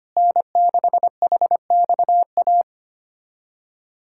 Morse code, heard as a keyed tone, about 25 wpm, 700 Hz. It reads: N6HXA
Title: USA Callsign DE Callsign - 5 Characters - 25wpm